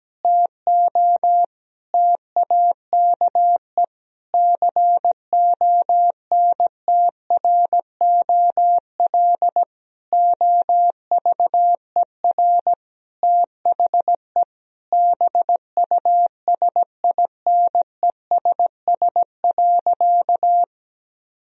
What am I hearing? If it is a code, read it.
Morse code, heard as a keyed tone, about 17 wpm, 700 Hz. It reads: TO TAKE CONTROL OVER THE BUSINESS.
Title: Sentences from Top 1000 Words - Review Part 3 - 17wpm